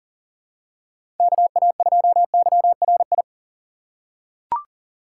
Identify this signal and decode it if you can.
Morse code, heard as a keyed tone, about 40 wpm, 700 Hz. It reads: KA2YRI
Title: USA Call Signs - 6x Character Spacing - 40wpm